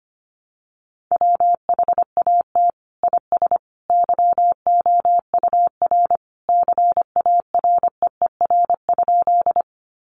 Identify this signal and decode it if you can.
Morse code, heard as a keyed tone, about 25 wpm, 700 Hz. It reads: WHAT IS YOUR CAREER?
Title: Sentences from Top 1000 Words 25wpm